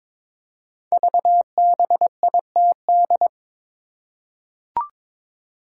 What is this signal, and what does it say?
Morse code, heard as a keyed tone, about 22 wpm, 700 Hz. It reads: VBITD
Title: Code Groups - 5 Letters - 22wpm